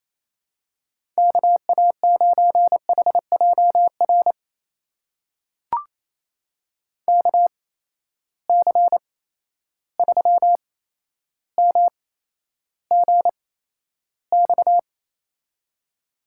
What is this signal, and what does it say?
Morse code, heard as a keyed tone, about 28 wpm, spaced as for 12 wpm, 700 Hz. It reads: KA9HJR KC3MGX
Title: USA Call Signs - 6 Characters - 8x Character Spacing - 28wpm